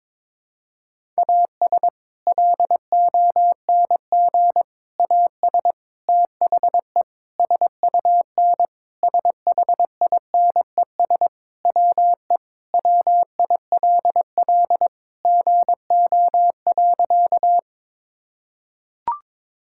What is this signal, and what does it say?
Morse code, heard as a keyed tone, about 22 wpm, 700 Hz. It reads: AS LONG AS THE SUN SHINES WE WILL GO.
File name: practice / 22wpm / Sentences with 4-Word Phrases - 22wpm